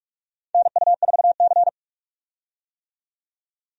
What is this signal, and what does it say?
Morse code, heard as a keyed tone, about 46 wpm, 700 Hz. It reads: NU4C
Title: USA Call Signs - 4 Characters - 45wpm